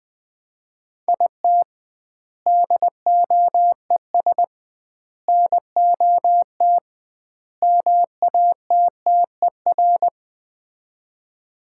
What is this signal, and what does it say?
Morse code, heard as a keyed tone, about 20 wpm, 700 Hz. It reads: IT DOES NOT MATTER
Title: 2x Word-Spacing - Sets of 4 Words - 20wpm